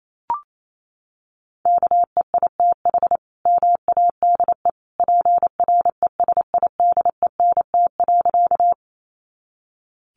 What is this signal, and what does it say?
Morse code, heard as a keyed tone, about 28 wpm, 700 Hz. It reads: KEITH MADE PRESIDENT.
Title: A Sound Of Thunder 28wpm